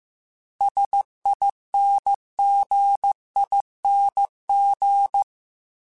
Morse code, heard as a keyed tone, about 15 wpm, 800 Hz. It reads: SINGING
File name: Special INGs / Special INGs-mp3s / Dble Ings @13